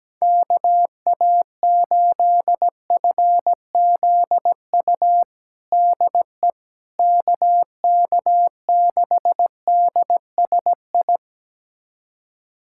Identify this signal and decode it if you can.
Morse code, heard as a keyed tone, about 17 wpm, 700 Hz. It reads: KA8FZU DE KK6DSI
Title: USA Callsign DE Callsign - 6 Characters - 17wpm